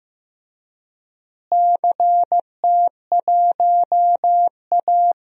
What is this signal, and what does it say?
Morse code, heard as a keyed tone, about 15 wpm, 700 Hz. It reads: CT1A
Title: DXCC-100 Call Signs - 4 Characters - Repeated 3x - 15wpm